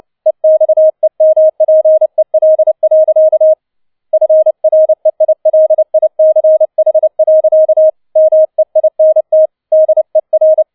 Morse code, heard as a keyed tone, about 29 wpm, 630 Hz. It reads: EXEMPEL. FREILICH. MEINT DER
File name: 007 Letzter Streich 30wpm